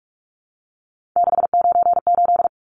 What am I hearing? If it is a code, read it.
Morse code, heard as a keyed tone, about 44 wpm, 700 Hz. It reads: 698
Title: Numbers - 3-Digit 45wpm